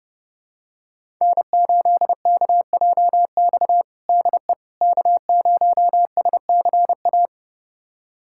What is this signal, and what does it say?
Morse code, heard as a keyed tone, about 30 wpm, 700 Hz. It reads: N8KJX DE K0SCA
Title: USA Callsign DE Callsign - 5 Characters - 30wpm